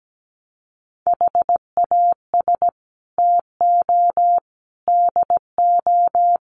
Morse code, heard as a keyed tone, about 17 wpm, 700 Hz. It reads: HAS TO DO
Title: Sets of 4 Words 17wpm